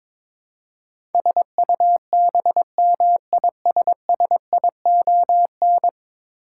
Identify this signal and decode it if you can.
Morse code, heard as a keyed tone, about 22 wpm, 700 Hz. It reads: SUBMISSION
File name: practice / 22wpm / Ten Letter Words - 22wpm